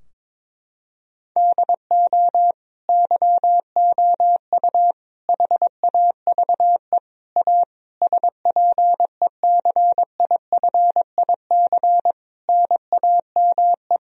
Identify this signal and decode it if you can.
Morse code, heard as a keyed tone, about 22 wpm, 700 Hz. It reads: DO YOU HAVE A SPECIFIC NAME
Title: Sentences from Top 1000 Words - Review Part 3 - 22wpm